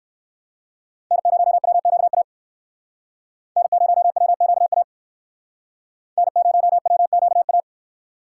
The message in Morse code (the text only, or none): N0WQA N0WQA N0WQA